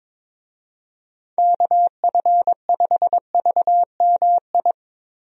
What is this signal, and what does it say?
Morse code, heard as a keyed tone, about 22 wpm, 700 Hz. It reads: KF5VMI
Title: USA Call Signs - 6 Characters - 22wpm